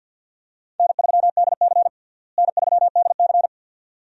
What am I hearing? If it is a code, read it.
Morse code, heard as a keyed tone, about 52 wpm, 700 Hz. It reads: N3DC N3DC
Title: USA Call Signs - 4 Characters - Repeated 2x - 50wpm